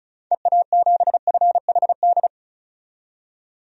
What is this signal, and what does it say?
Morse code, heard as a keyed tone, about 36 wpm, 700 Hz. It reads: EA7FHD
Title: DXCC-100 Call Signs - 6 Characters - 35wpm